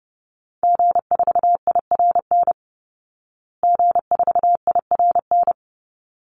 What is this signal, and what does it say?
Morse code, heard as a keyed tone, about 30 wpm, 700 Hz. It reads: G4IRN G4IRN
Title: Call Signs Repeated 3x - 30wpm